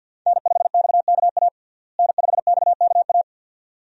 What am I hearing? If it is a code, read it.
Morse code, heard as a keyed tone, about 47 wpm, 700 Hz. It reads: N5XKA N5XKA
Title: USA Call Signs - 5 Characters - Repeated 2x - 50wpm